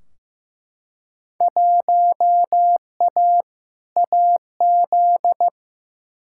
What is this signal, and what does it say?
Morse code, heard as a keyed tone, about 15 wpm, 700 Hz. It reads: 1A AZ
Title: Field Day Exchanges - 15wpm